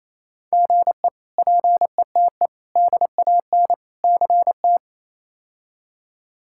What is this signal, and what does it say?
Morse code, heard as a keyed tone, about 27 wpm, 700 Hz. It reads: GE PETE DAN CT
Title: K1USN SST QSO Practice - 28wpm